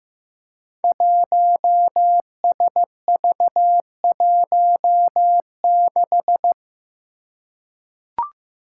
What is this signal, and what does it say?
Morse code, heard as a keyed tone, about 15 wpm, 700 Hz. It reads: 1SV16
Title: Five Challenging Letter-Number Combinations 15wpm